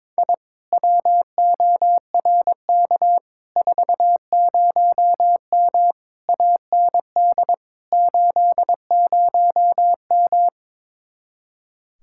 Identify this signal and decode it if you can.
Morse code, heard as a keyed tone, about 22 wpm, 700 Hz. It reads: I WORK 40M AND 80M
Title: Speed-Racing Common QSO Phrases - 22wpm